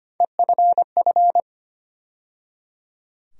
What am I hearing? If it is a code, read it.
Morse code, heard as a keyed tone, about 25 wpm, 700 Hz. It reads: EFF
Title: USA Call Sign Postfixes - 3 Letters 25wpm